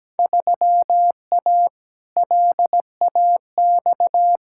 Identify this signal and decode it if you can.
Morse code, heard as a keyed tone, about 17 wpm, 700 Hz. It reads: 3A LAX
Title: Field Day Exchanges - 17wpm